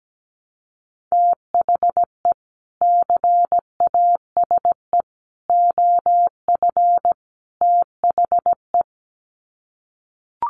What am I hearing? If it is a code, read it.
Morse code, heard as a keyed tone, about 17 wpm, 700 Hz. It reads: THE CASE OF THE
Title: Sets of 4 Words 17wpm